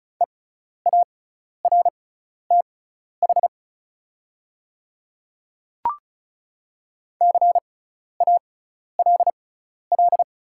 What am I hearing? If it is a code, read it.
Morse code, heard as a keyed tone, about 36 wpm, spaced as for 12 wpm, 700 Hz. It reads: EARTH CALL
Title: 6x Character Spacing - Top 200 Words - 35wpm